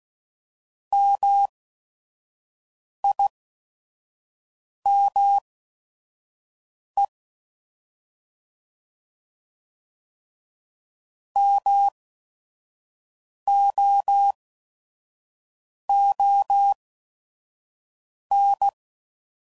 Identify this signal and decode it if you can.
Morse code, heard as a keyed tone, about 16 wpm, spaced as for 5 wpm, 800 Hz. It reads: MIME MOON